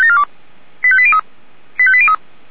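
Five-tone selective call (ZVEI), heard as quick runs of five tones, tones of about 70 ms. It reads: 85803 85803
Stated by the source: ZVEI five-tone call 85942 received on SDR.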